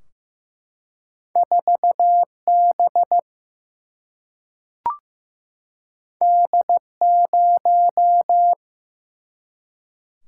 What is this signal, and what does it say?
Morse code, heard as a keyed tone, about 15 wpm, 700 Hz. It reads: 4 B D 0
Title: Single-Letter Single-Number 15wpm